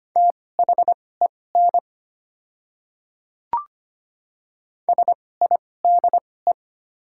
Morse code, heard as a keyed tone, about 25 wpm, spaced as for 18 wpm, 690 Hz. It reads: THEN SIDE
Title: 2x Character Spacing - Top 100 Words - 25wpm